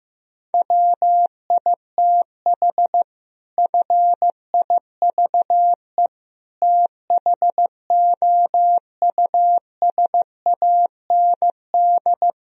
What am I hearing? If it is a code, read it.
Morse code, heard as a keyed tone, about 15 wpm, 700 Hz. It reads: WITH FIVE THOUSAND
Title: Sentences from Top 1000 Words - Review Part 3 - 15wpm